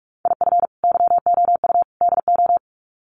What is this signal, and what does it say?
Morse code, heard as a keyed tone, about 44 wpm, 700 Hz. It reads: IF YOU DO
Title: Sets of 3 Words 45wpm